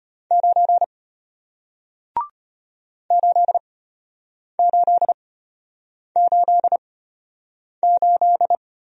Morse code, heard as a keyed tone, about 33 wpm, spaced as for 8 wpm, 700 Hz. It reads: 9 8888